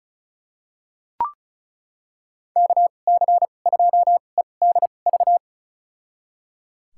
Morse code, heard as a keyed tone, about 36 wpm, spaced as for 25 wpm, 700 Hz. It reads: KC2EDV
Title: USA Call Signs - 6 Characters - 2x Character Spacing - 35wpm